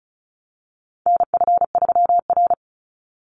Morse code, heard as a keyed tone, about 34 wpm, 700 Hz. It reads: NF3R
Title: Call Signs Repeated 2x - 35wpm